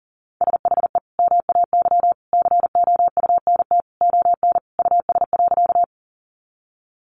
Morse code, heard as a keyed tone, about 40 wpm, 700 Hz. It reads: SHE MAY COUNT ON US.
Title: Sentences from Top 1500 Words 40wpm